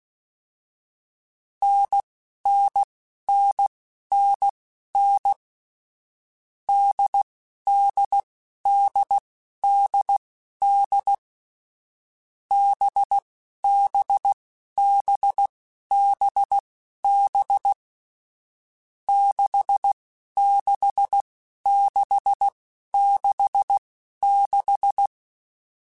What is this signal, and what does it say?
Morse code, heard as a keyed tone, about 16 wpm, spaced as for 11 wpm, 800 Hz. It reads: NNNNN DDDDD BBBBB 66666
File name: Short Runs / Short Runs-mp3s / NDB NDB6 @10